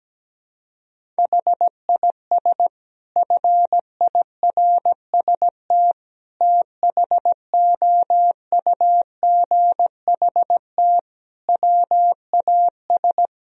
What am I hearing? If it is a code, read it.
Morse code, heard as a keyed tone, about 17 wpm, 700 Hz. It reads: HIS FIRST THOUGHT WAS
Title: Sentences from Top 1000 Words - Review Part 3 - 17wpm